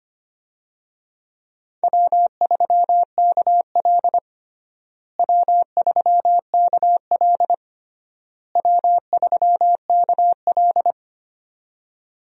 Morse code, heard as a keyed tone, about 25 wpm, 700 Hz. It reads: W3KL W3KL W3KL
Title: USA Call Signs - 4 Characters - Repeated 3x - 25wpm